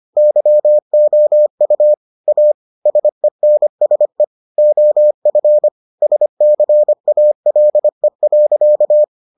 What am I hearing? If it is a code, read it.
Morse code, heard as a keyed tone, about 25 wpm, 600 Hz. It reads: YOU A SENSE OF SCALE.